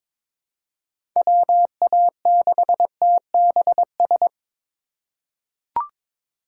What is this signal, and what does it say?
Morse code, heard as a keyed tone, about 22 wpm, 700 Hz. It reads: WA6TBS
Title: USA Call Signs - 2x Character Spacing - 22wpm